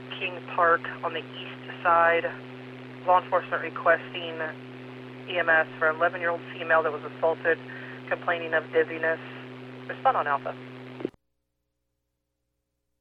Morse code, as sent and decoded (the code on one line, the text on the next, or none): none